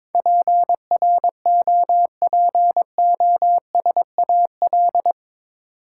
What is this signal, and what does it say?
Morse code, heard as a keyed tone, about 22 wpm, 700 Hz. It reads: PROPOSAL
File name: practice / 22wpm / Eight Letter Words - 22wpm